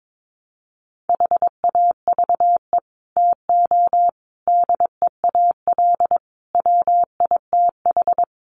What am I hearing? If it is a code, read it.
Morse code, heard as a keyed tone, about 22 wpm, 700 Hz. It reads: HAVE TO DEAL WITH
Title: Sets of 5 Words Encore2 22wpm